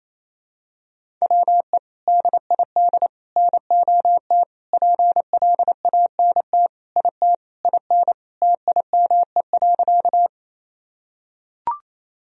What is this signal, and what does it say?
Morse code, heard as a keyed tone, about 28 wpm, 700 Hz. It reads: WE DID NOT PLANT IT IN TIME.
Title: Sentences from Top 1000 Words - Review Part 1 - 28wpm